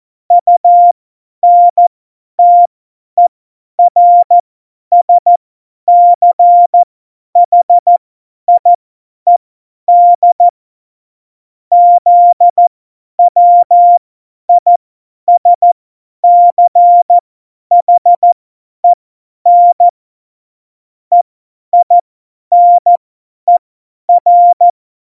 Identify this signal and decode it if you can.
Morse code, heard as a keyed tone, about 14 wpm, spaced as for 10 wpm, 700 Hz. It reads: UNTERSCHIED ZWISCHEN EINER
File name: Morsekurs / Witze / Witz 18 Juli 17, Tempo 50 F70